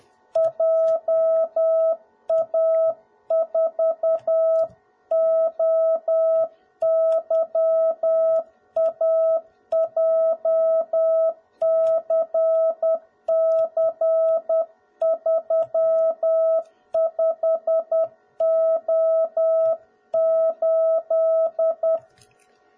Morse code, heard as a keyed tone, about 10 wpm, 650 Hz. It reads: JA4OYA JCC35O8
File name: jh8cbh / 2021cw / cw01b